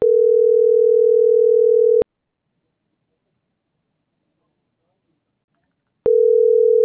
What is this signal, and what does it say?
Ringback tone (North America), 440 + 480 Hz.